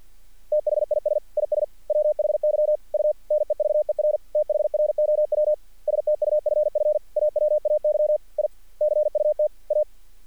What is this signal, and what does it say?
Morse code, heard as a keyed tone, about 50 wpm, 600 Hz. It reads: THIS IS WHY U NEVER THROW STUFF AWAY I CUT A
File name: archived / CRSnet